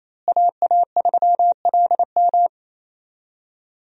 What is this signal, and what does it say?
Morse code, heard as a keyed tone, about 28 wpm, 700 Hz. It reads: AA3LM
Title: USA Call Signs - 5 Characters - Repeated 2x - 28wpm